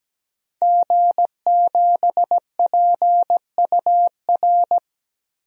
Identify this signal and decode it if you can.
Morse code, heard as a keyed tone, about 17 wpm, 700 Hz. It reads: G7PUR
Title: DXCC-100 Call Signs - 5 Characters - Repeated 3x - 17wpm